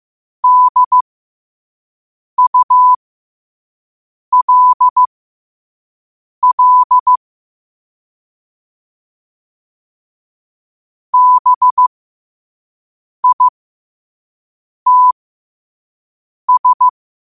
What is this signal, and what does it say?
Morse code, heard as a keyed tone, about 15 wpm, spaced as for 5 wpm, 1000 Hz. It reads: DULL BITS
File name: qotdfiles / 5wpm